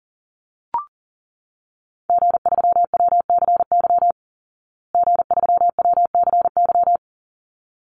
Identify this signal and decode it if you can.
Morse code, heard as a keyed tone, about 40 wpm, 700 Hz. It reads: G3WCY G3WCY
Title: Call Signs Repeated 3x - 40wpm